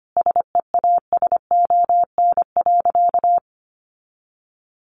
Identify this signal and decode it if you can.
Morse code, heard as a keyed tone, about 25 wpm, 700 Hz. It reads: SEASON.
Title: Sentences from Top 1400 Words 25wpm